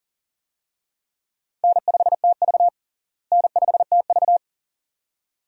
Morse code, heard as a keyed tone, about 40 wpm, 700 Hz. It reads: N5TV N5TV
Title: USA Call Signs - 4 Characters - Repeated 2x - 40wpm